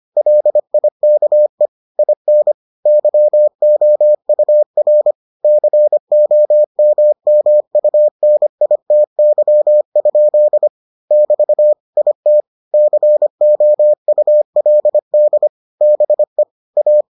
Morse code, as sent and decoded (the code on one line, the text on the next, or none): .-.. .. -.- . / .. -. / -.-- --- ..- .-. / -.-. --- -- -- ..- -. .. - -.-- ..--.. / -...- / .. - / -.-. --- ..- .-.. -.. / -... / .-
LIKE IN YOUR COMMUNITY? <BT> IT COULD BE A